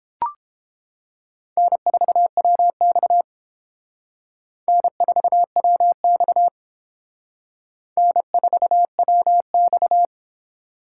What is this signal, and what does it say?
Morse code, heard as a keyed tone, about 28 wpm, 700 Hz. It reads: N4WX N4WX N4WX